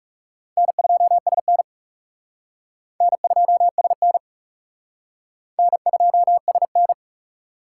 Morse code, heard as a keyed tone, about 36 wpm, 700 Hz. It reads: N2SN N2SN N2SN
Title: Speed Racing - USA 1x2 Call Signs 30wpm